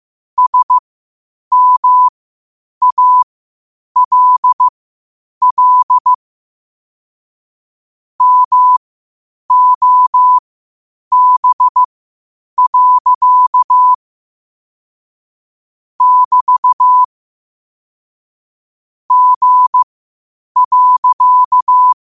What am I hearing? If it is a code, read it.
Morse code, heard as a keyed tone, about 15 wpm, spaced as for 8 wpm, 1000 Hz. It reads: SMALL MOB. <BT> G.
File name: qotdfiles / 8wpm